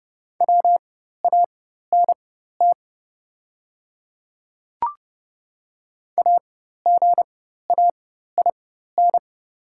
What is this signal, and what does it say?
Morse code, heard as a keyed tone, about 30 wpm, spaced as for 14 wpm, 700 Hz. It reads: WANT AGAIN
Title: 4x Character Spacing - Top 300 Words - 30wpm